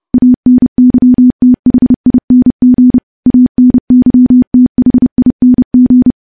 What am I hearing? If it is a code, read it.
Morse code, heard as a keyed tone, about 30 wpm, 250 Hz. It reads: ANYTHING ANYTHING
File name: morse-training-30WPM